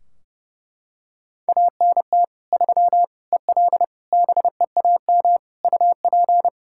Morse code, heard as a keyed tone, about 32 wpm, 700 Hz. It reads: ANT 3 EL BEAM UP
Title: Practice QSO - Exchange 2 - Info - 30wpm